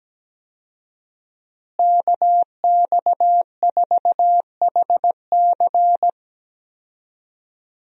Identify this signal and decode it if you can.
Morse code, heard as a keyed tone, about 17 wpm, 700 Hz. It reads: KX4HC
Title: USA Call Signs - 5 Characters - Repeated 2x - 17wpm